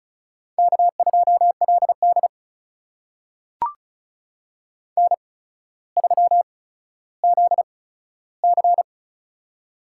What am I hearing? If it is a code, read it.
Morse code, heard as a keyed tone, about 34 wpm, spaced as for 15 wpm, 700 Hz. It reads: K2LD N3ZC
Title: USA Call Signs - 4 Characters - 8x Character Spacing - 35wpm